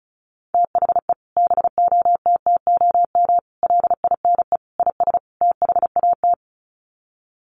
Morse code, heard as a keyed tone, about 34 wpm, 700 Hz. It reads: THE BOTTOM LINE IS THAT
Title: Sets of 5 Words Encore 35wpm